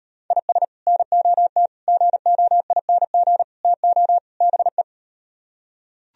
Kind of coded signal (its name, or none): Morse code